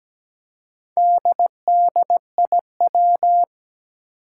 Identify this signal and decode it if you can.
Morse code, heard as a keyed tone, about 17 wpm, 700 Hz. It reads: DDIW
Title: Four Challenging Letter-Number Combinations 17wpm